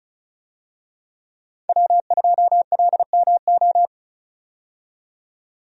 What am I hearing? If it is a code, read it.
Morse code, heard as a keyed tone, about 36 wpm, 700 Hz. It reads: W2LMO
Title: USA Call Signs - 5 Characters - 35wpm